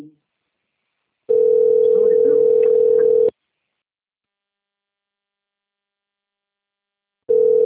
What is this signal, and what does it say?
Ringback tone (North America), 440 + 480 Hz, with speech.
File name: rg-1600-unknown- - - .86011